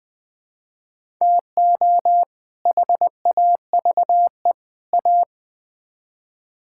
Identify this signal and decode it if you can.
Morse code, heard as a keyed tone, about 20 wpm, 700 Hz. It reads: TO HAVE A
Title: Build A Phrase - 3-Word Phrases - 20wpm